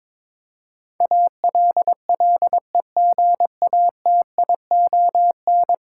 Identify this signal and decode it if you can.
Morse code, heard as a keyed tone, about 22 wpm, 700 Hz. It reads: ALLEGATION
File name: practice / 22wpm / 10 Letter Words - First 3 Letters Given - 22wpm